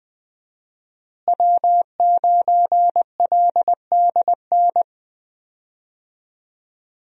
Morse code, heard as a keyed tone, about 20 wpm, 700 Hz. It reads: W9LDN
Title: USA Call Signs - 5 Characters - Repeated 3x - 20wpm